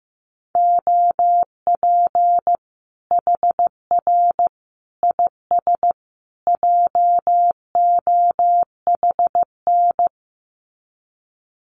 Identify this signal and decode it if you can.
Morse code, heard as a keyed tone, about 15 wpm, 700 Hz. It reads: OP HR IS JOHN
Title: Common QSO Phrases 15wpm